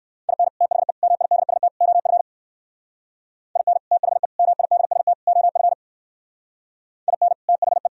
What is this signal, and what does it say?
Morse code, heard as a keyed tone, about 58 wpm, 700 Hz. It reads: IN THE MIDST OF IN THE MIDST OF IN THE